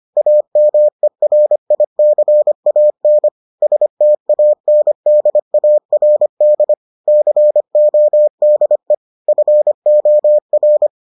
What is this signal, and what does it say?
Morse code, heard as a keyed tone, about 25 wpm, 600 Hz. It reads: AMERICAN STANDARD CODE FOR